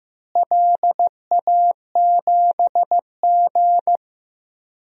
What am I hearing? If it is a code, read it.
Morse code, heard as a keyed tone, about 15 wpm, 700 Hz. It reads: LA7G
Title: DXCC-100 Call Signs - 4 Characters - Repeated 3x - 15wpm